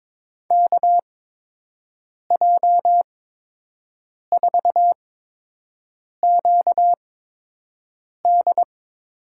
Morse code, heard as a keyed tone, about 22 wpm, spaced as for 6 wpm, 700 Hz. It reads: KJ4QD
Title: USA Call Signs - 6 Characters - 8x Character Spacing - 22wpm